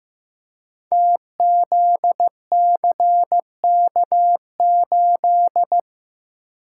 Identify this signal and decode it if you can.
Morse code, heard as a keyed tone, about 15 wpm, 700 Hz. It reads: TZCK8
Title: Code Groups - 5 Letter:Numbers - 15wpm